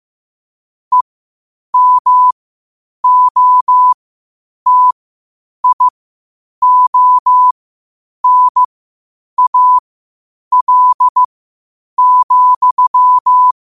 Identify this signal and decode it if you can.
Morse code, heard as a keyed tone, about 15 wpm, spaced as for 8 wpm, 1000 Hz. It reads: EMOTIONAL,
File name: qotdfiles / 8wpm